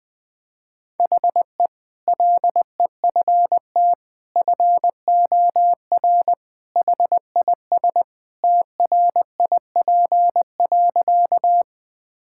Morse code, heard as a keyed tone, about 20 wpm, 700 Hz. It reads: HE LEFT FOR HIS TRIP.
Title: Sentences with 2-Word Phrases - 20wpm